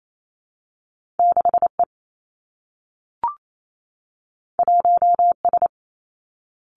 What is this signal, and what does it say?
Morse code, heard as a keyed tone, about 28 wpm, 700 Hz. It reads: 6 E 1 S
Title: Two Challenging Letter-Number Combinations 28wpm